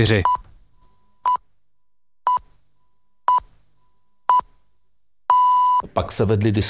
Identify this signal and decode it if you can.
Radio time signal (time pips), with speech. It is heard from 0.2 to 5.8 s.